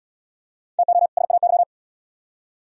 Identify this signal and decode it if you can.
Morse code, heard as a keyed tone, about 75 wpm, 700 Hz. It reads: TO SAY